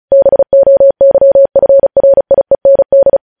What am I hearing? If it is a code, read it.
Morse code, heard as a keyed tone, about 36 wpm, 550 Hz. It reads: BOYFRIEND